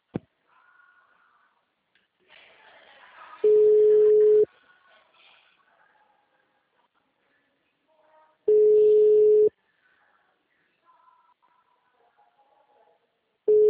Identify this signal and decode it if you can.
Ringback tone (Europe, 425 Hz).
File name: out- -2009- - - .75625